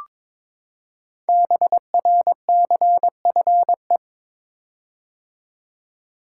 Morse code, heard as a keyed tone, about 23 wpm, 700 Hz. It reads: BRCFE
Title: Code Groups - 5 Letters - 22wpm